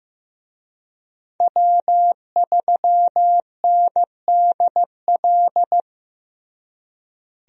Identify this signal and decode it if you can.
Morse code, heard as a keyed tone, about 15 wpm, 700 Hz. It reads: W3NDL
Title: USA Call Signs - 5 Characters - Repeated 3x - 15wpm